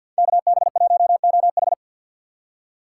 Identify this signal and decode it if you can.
Morse code, heard as a keyed tone, about 62 wpm, 700 Hz. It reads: KB1OH
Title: Speed Racing - USA 2x2 Call Signs 50wpm